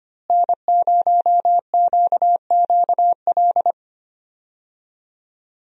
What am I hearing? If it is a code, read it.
Morse code, heard as a keyed tone, about 25 wpm, 700 Hz. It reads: N0QQL
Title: USA Call Signs - 5 Characters - Repeated 3x - 25wpm